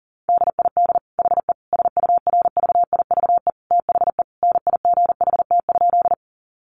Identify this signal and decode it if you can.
Morse code, heard as a keyed tone, about 37 wpm, 700 Hz. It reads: DID HE SURVIVE THE NIGHT?
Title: Sentences from Top 1300 Words 40wpm